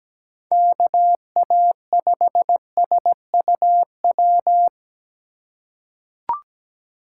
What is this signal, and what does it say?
Morse code, heard as a keyed tone, about 17 wpm, 700 Hz. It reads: KA5SUW
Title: USA Call Signs - 6 Characters - 4x Character Spacing - 17wpm